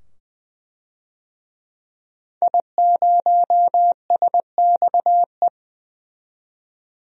Morse code, heard as a keyed tone, about 21 wpm, 700 Hz. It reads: I0SXE
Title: DXCC-100 Call Signs - 5 Characters - 20wpm